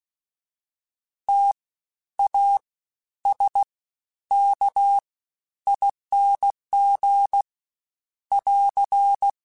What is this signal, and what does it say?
Morse code, heard as a keyed tone, about 16 wpm, spaced as for 10 wpm, 800 Hz. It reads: TASKING <AR>
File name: Special INGs-mp3s / Special INGs-mp3s / Fast ing @15